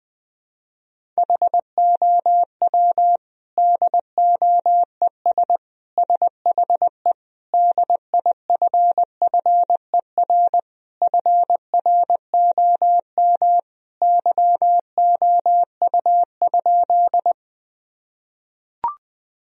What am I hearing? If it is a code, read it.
Morse code, heard as a keyed tone, about 20 wpm, 700 Hz. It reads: HOW DOES SHE DIFFER FROM YOU?
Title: Sentences from Top 1000 Words - Review Part 1 - 20wpm